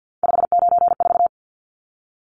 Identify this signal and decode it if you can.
Morse code, heard as a keyed tone, about 50 wpm, 700 Hz. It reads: 594
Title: Numbers - 3-Digit 50wpm